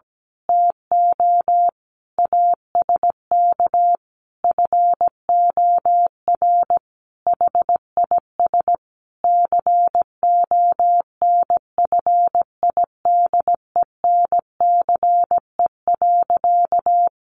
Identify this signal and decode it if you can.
Morse code, heard as a keyed tone, about 17 wpm, 700 Hz. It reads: TO ASK FOR HIS CONFIDENCE.
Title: Sentences from Top 1500 Words 17wpm